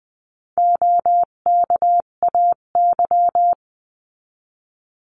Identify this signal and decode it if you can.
Morse code, heard as a keyed tone, about 20 wpm, 700 Hz. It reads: OKAY
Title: 258 Lesson 42c - Words 16-20wpm